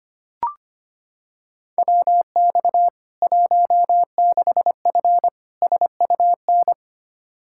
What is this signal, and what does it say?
Morse code, heard as a keyed tone, about 25 wpm, 700 Hz. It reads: WX 16F SUN
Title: K1USN SST QSO Practice 25wpm